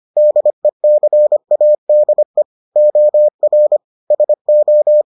Morse code, heard as a keyed tone, about 25 wpm, 600 Hz. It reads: DECADE OR SO